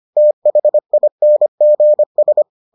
Morse code, heard as a keyed tone, about 25 wpm, 600 Hz. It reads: THINGS